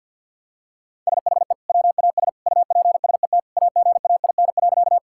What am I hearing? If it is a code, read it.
Morse code, heard as a keyed tone, about 51 wpm, 700 Hz. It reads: SHE WAS UPSET AGAIN.
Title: Sentences from Top 1000 Words - Review Part 5 - 50wpm